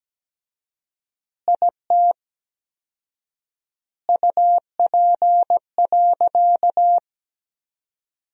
Morse code, heard as a keyed tone, about 17 wpm, 700 Hz. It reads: IT UP.
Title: 4x Word-Spacing - Sentences from Top 100 Words - 17wpm